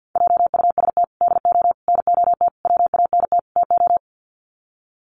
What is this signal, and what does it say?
Morse code, heard as a keyed tone, about 50 wpm, 700 Hz. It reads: JUST DO NOT WANT TO
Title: Sets of 5 Words Encore 50wpm